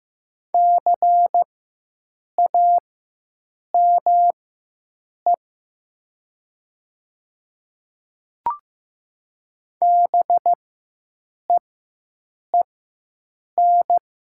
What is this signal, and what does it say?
Morse code, heard as a keyed tone, about 15 wpm, spaced as for 7 wpm, 700 Hz. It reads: CAME BEEN
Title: 4x Character Spacing - Top 200 Words - 15wpm